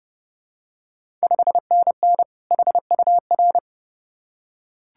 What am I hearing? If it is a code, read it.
Morse code, heard as a keyed tone, about 30 wpm, 700 Hz. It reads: 5NN HUR
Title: Ontario QSO Party Exchange 30wpm